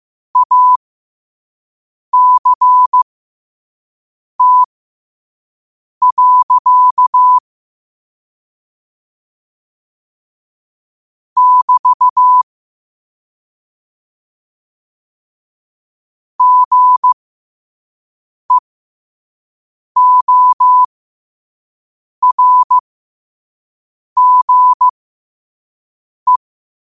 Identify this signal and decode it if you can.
Morse code, heard as a keyed tone, about 15 wpm, spaced as for 5 wpm, 1000 Hz. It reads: ACT. <BT> GEORGE
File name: 5wpm